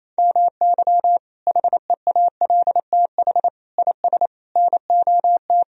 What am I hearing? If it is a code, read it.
Morse code, heard as a keyed tone, about 28 wpm, 700 Hz. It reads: MY HEALTH IS NOT
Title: Sentences from Top 1000 Words - Review Part 3 - 28wpm